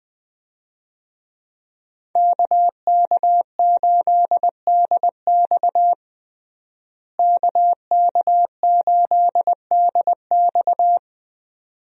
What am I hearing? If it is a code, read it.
Morse code, heard as a keyed tone, about 20 wpm, 700 Hz. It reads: KK8DX KK8DX
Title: USA Call Signs - 5 Characters - Repeated 2x - 20wpm